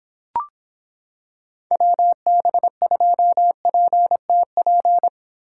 Morse code, heard as a keyed tone, about 25 wpm, 700 Hz. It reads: WB2PTP